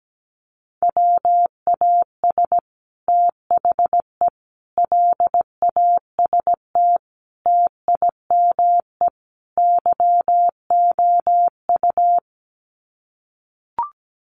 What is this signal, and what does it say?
Morse code, heard as a keyed tone, about 17 wpm, 700 Hz. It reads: WAS THE LAST TIME YOU
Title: Sets of 5 Words Encore2 17wpm